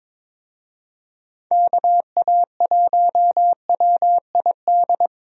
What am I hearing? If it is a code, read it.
Morse code, heard as a keyed tone, about 22 wpm, 700 Hz. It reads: KA1WID
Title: USA Call Signs - 6 Characters - Repeated 3x - 22wpm